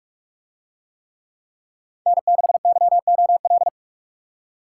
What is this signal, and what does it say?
Morse code, heard as a keyed tone, about 46 wpm, 700 Hz. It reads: N6YOL